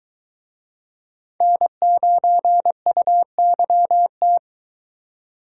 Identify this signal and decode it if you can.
Morse code, heard as a keyed tone, about 24 wpm, 700 Hz. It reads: N9UYT